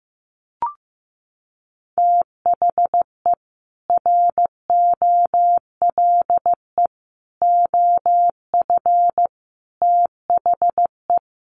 Morse code, heard as a keyed tone, about 15 wpm, 700 Hz. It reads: THE ROLE OF THE
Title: Sets of 4 Words 15wpm